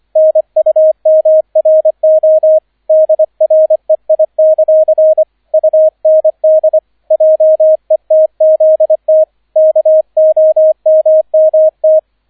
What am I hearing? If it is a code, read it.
Morse code, heard as a keyed tone, about 25 wpm, 630 Hz, with radio noise. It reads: NUMRO DREI; UND JETZT KOMMT